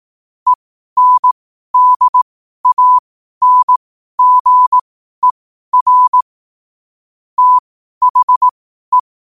Morse code, heard as a keyed tone, about 18 wpm, spaced as for 12 wpm, 1000 Hz. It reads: ENDANGER THE
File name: qotdfiles / 12wpm